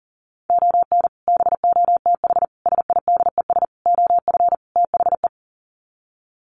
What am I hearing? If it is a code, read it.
Morse code, heard as a keyed tone, about 40 wpm, 700 Hz. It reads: ON BOTH SIDES OF THE
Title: Sets of 5 Words 40wpm